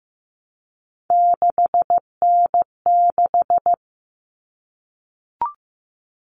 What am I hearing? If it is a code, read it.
Morse code, heard as a keyed tone, about 15 wpm, 700 Hz. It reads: 6N6
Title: All Cut Numbers - 3-Digit 15wpm